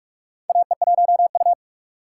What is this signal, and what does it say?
Morse code, heard as a keyed tone, about 44 wpm, 690 Hz. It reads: AE1U
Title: USA Call Signs - 4 Characters - 4x Character Spacing - 45wpm